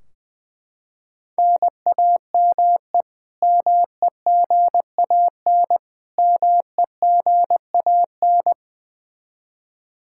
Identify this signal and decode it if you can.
Morse code, heard as a keyed tone, about 20 wpm, 700 Hz. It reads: NAME MEGAN MEGAN
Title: Practice QSO - Exchange 1 - Beginning - 20wpm